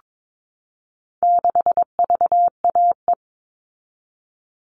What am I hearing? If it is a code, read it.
Morse code, heard as a keyed tone, about 22 wpm, 700 Hz. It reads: 6VAE
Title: All Cut Numbers - 4-Digit 22wpm